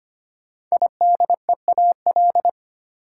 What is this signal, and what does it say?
Morse code, heard as a keyed tone, about 25 wpm, 700 Hz. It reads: IDEAL
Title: Sentences from Top 1400 Words - New Word Drill - 25wpm